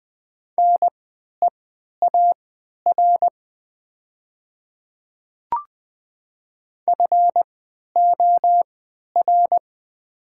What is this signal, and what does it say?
Morse code, heard as a keyed tone, about 20 wpm, spaced as for 11 wpm, 700 Hz. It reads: NEAR FOR
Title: 3x Character Spacing - Top 200 Words - 20wpm